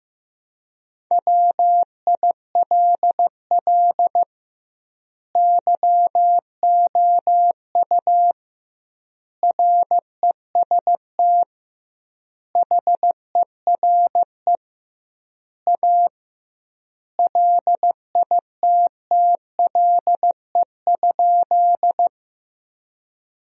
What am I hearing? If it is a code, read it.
Morse code, heard as a keyed tone, about 16 wpm, 700 Hz. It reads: WILL YOU REST HERE A LITTLE?
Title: 2x Word-Spacing - Sentences from Top 500 Words - 15wpm